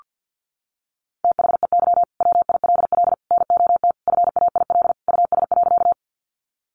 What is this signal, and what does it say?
Morse code, heard as a keyed tone, about 48 wpm, 700 Hz. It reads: THEY WILL NOT FAIL US.
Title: Sentences from Top 1100 Words 50wpm